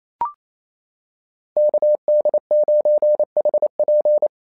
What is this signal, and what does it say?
Morse code, heard as a keyed tone, about 27 wpm, 690 Hz. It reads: KD9HP
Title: SOTA QSO Practice 28wpm